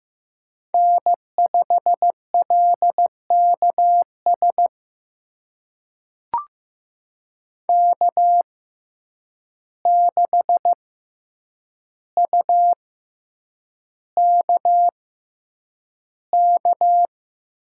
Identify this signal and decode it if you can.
Morse code, heard as a keyed tone, about 15 wpm, spaced as for 8 wpm, 700 Hz. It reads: N5LKS K6UKK
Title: USA Call Signs - 6x Character Spacing - 15wpm